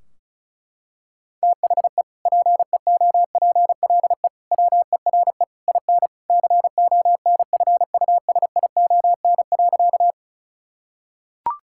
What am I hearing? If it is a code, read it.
Morse code, heard as a keyed tone, about 36 wpm, 700 Hz. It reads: THE PEOPLE WERE IN CONFUSION.
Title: Sentences from Top 1600 Words 35wpm